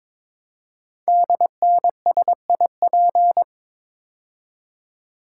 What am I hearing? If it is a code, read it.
Morse code, heard as a keyed tone, about 23 wpm, 700 Hz. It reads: DNSIP
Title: Code Groups - 5 Letters - 22wpm